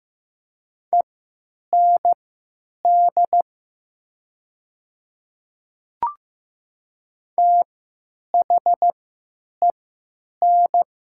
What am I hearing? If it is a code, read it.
Morse code, heard as a keyed tone, about 15 wpm, spaced as for 9 wpm, 700 Hz. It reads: END THEN
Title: 3x Character Spacing - Top 200 Words - 15wpm